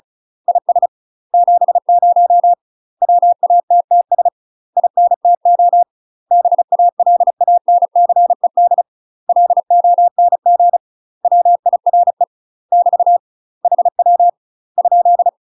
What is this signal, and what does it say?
Morse code, heard as a keyed tone, about 34 wpm, 700 Hz. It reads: IS 70 WATTS INTO BALANCED LONG WIRE <BT> HW ?
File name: UK MASTER QSOs TRAINING / 35WPM / UK MASTER QSOs 0039